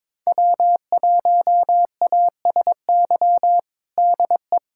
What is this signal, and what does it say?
Morse code, heard as a keyed tone, about 22 wpm, 700 Hz. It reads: W1ASY DE
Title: USA Callsign DE Callsign - 5 Characters - 22wpm